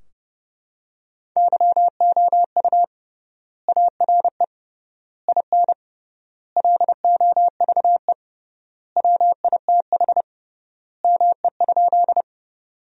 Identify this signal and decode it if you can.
Morse code, heard as a keyed tone, about 32 wpm, 700 Hz. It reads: YOU ARE IN LOVE WITH ME?
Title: 3x Word-Spacing - Sentences from Top 400 Words - 30wpm